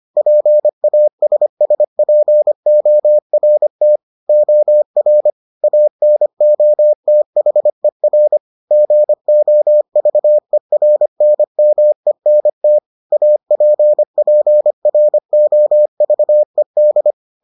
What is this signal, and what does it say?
Morse code, heard as a keyed tone, about 25 wpm, 600 Hz. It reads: PASSPORT OR ANOTHER GOVERNMENT APPROVED